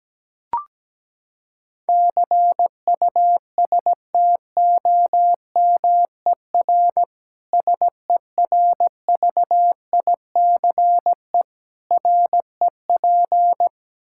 Morse code, heard as a keyed tone, about 17 wpm, 700 Hz. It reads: CUSTOMER SERVICE REP
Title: K1USN SST QSO Practice 17wpm